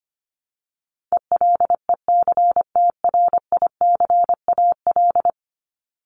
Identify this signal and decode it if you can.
Morse code, heard as a keyed tone, about 25 wpm, 700 Hz. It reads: ELECTRICAL
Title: Top 751-1000 Words 25wpm